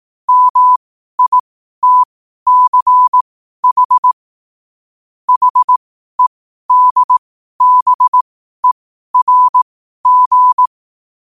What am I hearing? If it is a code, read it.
Morse code, heard as a keyed tone, about 18 wpm, spaced as for 12 wpm, 1000 Hz. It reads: MITCH HEDBERG
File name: qotdfiles / 12wpm